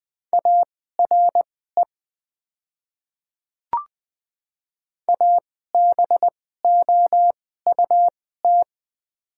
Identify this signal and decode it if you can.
Morse code, heard as a keyed tone, about 20 wpm, spaced as for 14 wpm, 700 Hz. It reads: ARE ABOUT
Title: 2x Character Spacing - Top 200 Words - 20wpm